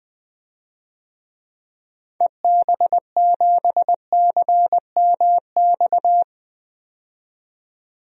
Morse code, heard as a keyed tone, about 20 wpm, 700 Hz. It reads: EB7CMX
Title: DXCC-100 Call Signs - 6 Characters - 20wpm